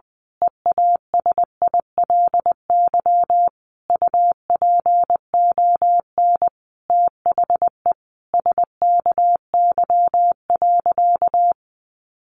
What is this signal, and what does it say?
Morse code, heard as a keyed tone, about 20 wpm, 700 Hz. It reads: EASILY UPON THE SKY.
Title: A Sound Of Thunder 20wpm